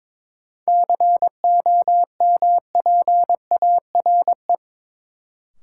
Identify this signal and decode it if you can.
Morse code, heard as a keyed tone, about 23 wpm, 700 Hz. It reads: COMPARE
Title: Sentences from Top 1300 Words - New Word Drill - 22wpm